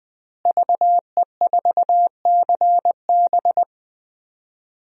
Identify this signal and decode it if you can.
Morse code, heard as a keyed tone, about 20 wpm, 700 Hz. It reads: VE4CB
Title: DXCC-100 Call Signs - 5 Characters - Repeated 3x - 20wpm